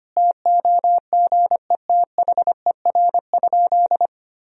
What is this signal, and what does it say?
Morse code, heard as a keyed tone, about 25 wpm, 700 Hz. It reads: TOGETHER?
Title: Sentences from Top 1000 Words - Review Part 4 - 25wpm